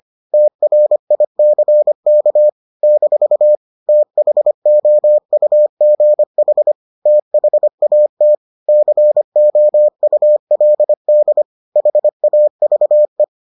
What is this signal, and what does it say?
Morse code, heard as a keyed tone, about 25 wpm, 600 Hz. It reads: TRICK - THOUGH THAT COULD HAVE